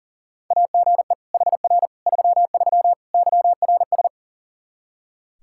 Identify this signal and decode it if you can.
Morse code, heard as a keyed tone, about 40 wpm, 700 Hz. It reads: AGE HR 33 YRS
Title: Practice QSO - Exchange 3 - More Info - 40wpm